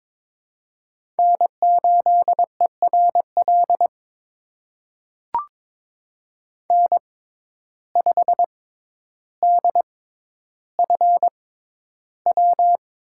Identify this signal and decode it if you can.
Morse code, heard as a keyed tone, about 23 wpm, spaced as for 11 wpm, 700 Hz. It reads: N8ERL N5DFW
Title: USA Call Signs - 5 Characters - 6x Character Spacing - 22wpm